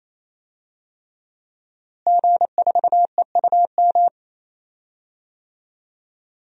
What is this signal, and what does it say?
Morse code, heard as a keyed tone, about 28 wpm, 700 Hz. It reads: G4EUM
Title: DXCC-100 Call Signs - 5 Characters - Repeated 3x - 28wpm